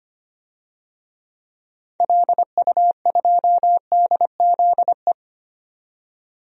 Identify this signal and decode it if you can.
Morse code, heard as a keyed tone, about 25 wpm, 700 Hz. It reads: LU2DZE
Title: DXCC-100 Call Signs - 6 Characters - 25wpm